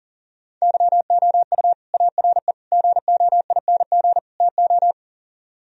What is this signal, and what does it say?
Morse code, heard as a keyed tone, about 50 wpm, 700 Hz. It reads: YOU ARE GOING TO
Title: Speed-Racing - Sets of 4 Words 40wpm